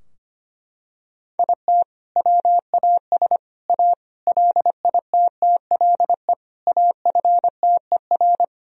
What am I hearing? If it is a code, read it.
Morse code, heard as a keyed tone, about 25 wpm, 700 Hz. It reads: IT WAS A LITTLE AFTER
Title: Sentences from Top 1000 Words - Review Part 3 - 25wpm